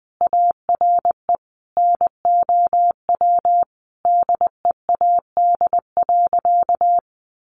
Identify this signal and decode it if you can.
Morse code, heard as a keyed tone, about 20 wpm, 700 Hz. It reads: ARE NOW DEAD.
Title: Sentences from Top 1500 Words 20wpm